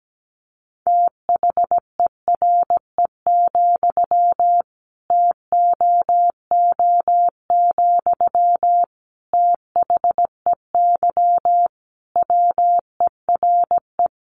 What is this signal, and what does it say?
Morse code, heard as a keyed tone, about 17 wpm, 700 Hz. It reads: THERE, TOO, THEY WERE
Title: Sentences from Top 1200 Words 17wpm